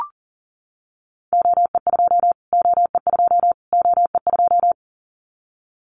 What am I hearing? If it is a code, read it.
Morse code, heard as a keyed tone, about 40 wpm, 700 Hz. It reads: OE2 OE2 OE2
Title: Callsign Prefixes for Top 25 DXCC - Repeated 3x - 40wpm